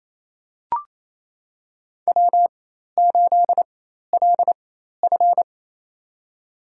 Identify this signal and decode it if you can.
Morse code, heard as a keyed tone, about 28 wpm, spaced as for 13 wpm, 700 Hz. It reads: W8LF
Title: USA Call Signs - 4 Characters - 4x Character Spacing - 28wpm